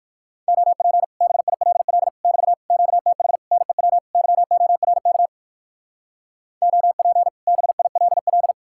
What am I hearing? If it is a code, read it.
Morse code, heard as a keyed tone, about 55 wpm, 700 Hz. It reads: OP BILL <BT> QTH NEW YORK OP BILL
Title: Speed-Racing Common QSO Phrases Encore - 45wpm